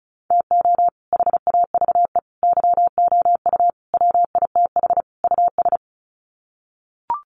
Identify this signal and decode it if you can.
Morse code, heard as a keyed tone, about 34 wpm, 700 Hz. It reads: TO HAVE YOU WITH US
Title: Sets of 5 Words 35wpm